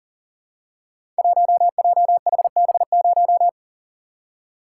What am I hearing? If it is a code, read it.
Morse code, heard as a keyed tone, about 40 wpm, 700 Hz. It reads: 1JHB0
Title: Code Groups - 5 Letter:Numbers - 40wpm